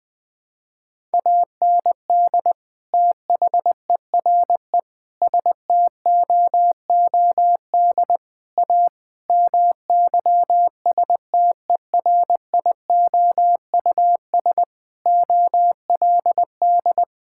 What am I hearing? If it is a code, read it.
Morse code, heard as a keyed tone, about 20 wpm, 700 Hz. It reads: AND THERE STOOD A MYSTERIOUS OLD
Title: Sentences with 2-Word Phrases - Spoken Beginning - 20wpm